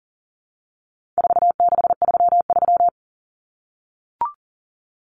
Morse code, heard as a keyed tone, about 40 wpm, 700 Hz. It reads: 4633
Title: Common Cut Numbers - 4-Digit 40wpm